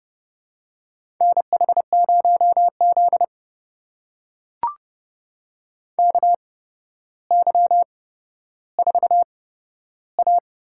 Morse code, heard as a keyed tone, about 30 wpm, spaced as for 13 wpm, 700 Hz. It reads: NH0Z KY4A
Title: USA Call Signs - 4 Characters - 8x Character Spacing - 30wpm